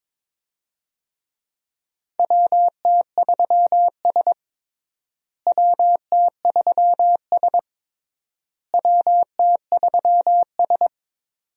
Morse code, heard as a keyed tone, about 22 wpm, 700 Hz. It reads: WT3S WT3S WT3S
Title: USA Call Signs - 4 Characters - Repeated 3x - 22wpm